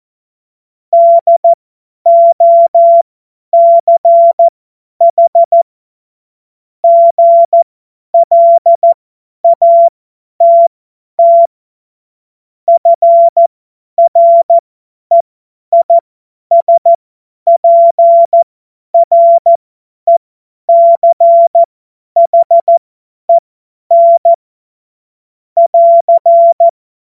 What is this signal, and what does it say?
Morse code, heard as a keyed tone, about 14 wpm, spaced as for 10 wpm, 680 Hz. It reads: DOCH GLATT FREISPRECHEN <AR>
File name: Morsekurs / Witze / Witz 03. Mai 2022 50 F70